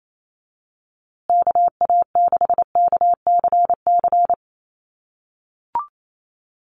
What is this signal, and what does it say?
Morse code, heard as a keyed tone, about 28 wpm, 700 Hz. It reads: KA6KCC
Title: USA Call Signs Repeated 2x 28wpm